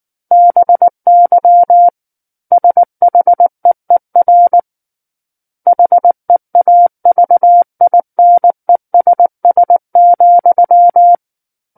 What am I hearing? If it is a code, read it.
Morse code, heard as a keyed tone, about 19 wpm, 700 Hz. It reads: BY SHEER HEAVINESS,
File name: Book One / 10-Chapter 10 - 19 WPM